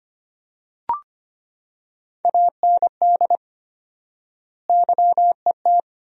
Morse code, heard as a keyed tone, about 25 wpm, 700 Hz. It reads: AND YET
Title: 4x Word-Spacing - Sentences from Top 500 Words - 25wpm